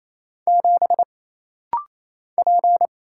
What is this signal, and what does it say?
Morse code, heard as a keyed tone, about 28 wpm, 700 Hz. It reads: 7 P
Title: Single Letter-Number - Rapid-Fire 28wpm